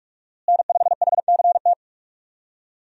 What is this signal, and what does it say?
Morse code, heard as a keyed tone, about 42 wpm, 700 Hz. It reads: N5HCT
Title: USA Call Signs - 5 Characters - Repeated 3x - 45wpm